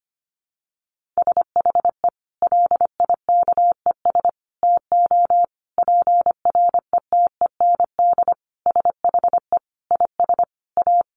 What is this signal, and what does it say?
Morse code, heard as a keyed tone, about 25 wpm, 700 Hz. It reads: SHE LIKES TO PRETEND SHE IS A
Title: Additional Easy Phrases 25wpm